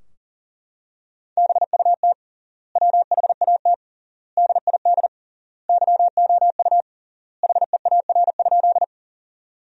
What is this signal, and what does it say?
Morse code, heard as a keyed tone, about 40 wpm, 700 Hz. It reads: BUT WHAT DID YOU HEAR?
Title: 3x Word-Spacing - Sentences from Top 300 Words - 40wpm